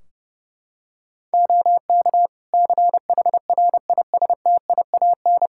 Morse code, heard as a keyed tone, about 30 wpm, 700 Hz. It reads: OK CHRISTIAN
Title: Practice QSO - Exchange 4 - Ending - 30wpm